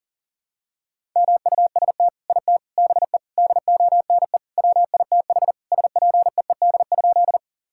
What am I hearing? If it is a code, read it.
Morse code, heard as a keyed tone, about 40 wpm, 700 Hz. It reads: MUST IT BE DONE WITH SPEED?
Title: Sentences from Top 1000 Words - Review Part 5 - 40wpm